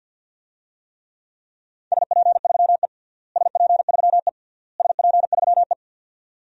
Morse code, heard as a keyed tone, about 50 wpm, 700 Hz. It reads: SP3E SP3E SP3E
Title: DXCC-100 Call Signs - 4 Characters - Repeated 3x - 50wpm